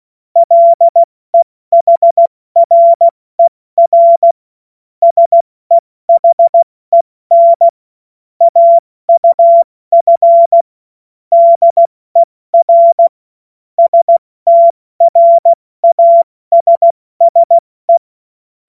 Morse code, heard as a keyed tone, about 16 wpm, 680 Hz. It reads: LEHRER SEHEN AUF DER STRASSE
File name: Morsekurs / Witze / Witz 18. April 70 F80